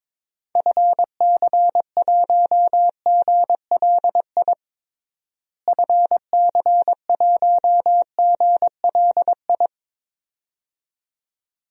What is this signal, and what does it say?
Morse code, heard as a keyed tone, about 22 wpm, 700 Hz. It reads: FC1GLI FC1GLI
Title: DXCC-100 Call Signs - 6 Characters - Repeated 2x - 22wpm